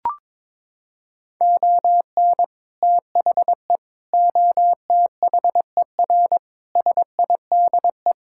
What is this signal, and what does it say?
Morse code, heard as a keyed tone, about 22 wpm, 700 Hz. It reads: ON THE OTHER SIDE
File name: practice / 22wpm / Sentences with 2-Word Phrases - 22wpm